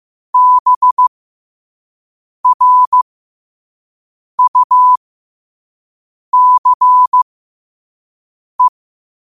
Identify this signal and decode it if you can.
Morse code, heard as a keyed tone, about 15 wpm, spaced as for 5 wpm, 1000 Hz. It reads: BRUCE
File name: qotdfiles / 5wpm